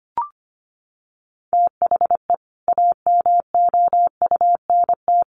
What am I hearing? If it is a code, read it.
Morse code, heard as a keyed tone, about 25 wpm, 700 Hz. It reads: THE AMOUNT
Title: Sets of 3 Words 25wpm